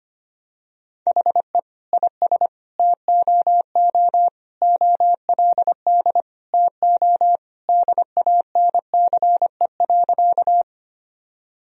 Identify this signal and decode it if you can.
Morse code, heard as a keyed tone, about 25 wpm, 700 Hz. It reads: HE IS TOO OLD TO DANCE.
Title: Sentences from Top 1000 Words - Review Part 4 - 25wpm